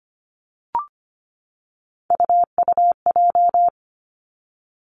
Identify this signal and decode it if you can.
Morse code, heard as a keyed tone, about 25 wpm, 700 Hz. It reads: UUJ
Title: Three Challenging Letter-Number Combinations 25wpm